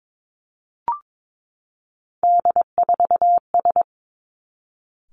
Morse code, heard as a keyed tone, about 22 wpm, 700 Hz. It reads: D4S
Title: Three Challenging Letter-Number Combinations 22wpm